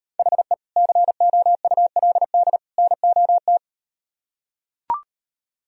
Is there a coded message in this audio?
Yes: Morse code from 0.2 to 3.6 s, heard as a keyed tone, about 38 wpm, 700 Hz. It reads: HE COULD NOT